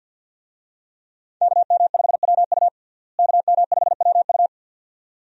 Morse code, heard as a keyed tone, about 51 wpm, 700 Hz. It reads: KM5WU KM5WU
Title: USA Call Signs - 5 Characters - Repeated 2x - 50wpm